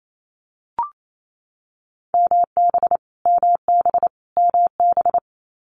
Morse code, heard as a keyed tone, about 28 wpm, 700 Hz. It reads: MB MB MB
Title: Multi-letter Phonemes Repeated 3x 28wpm